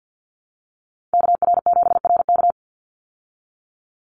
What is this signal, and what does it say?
Morse code, heard as a keyed tone, about 50 wpm, 700 Hz. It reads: KR7RK
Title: Call Signs Repeated 2x - 50wpm